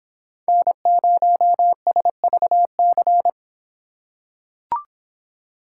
Morse code, heard as a keyed tone, about 25 wpm, 700 Hz. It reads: N0SVC